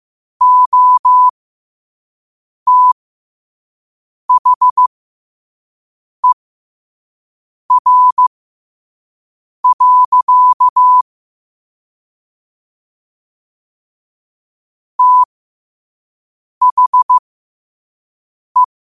Morse code, heard as a keyed tone, about 15 wpm, spaced as for 5 wpm, 1000 Hz. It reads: OTHER. THE
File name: qotdfiles / 5wpm